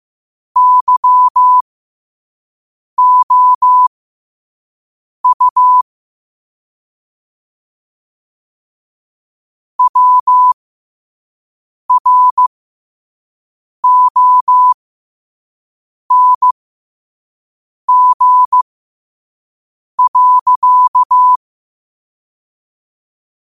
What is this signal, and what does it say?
Morse code, heard as a keyed tone, about 15 wpm, spaced as for 5 wpm, 1000 Hz. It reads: YOU WRONG.
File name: qotdfiles / 5wpm